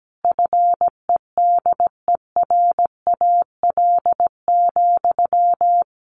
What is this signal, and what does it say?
Morse code, heard as a keyed tone, about 17 wpm, 700 Hz. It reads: FEDERAL,
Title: Sentences from Top 1200 Words 17wpm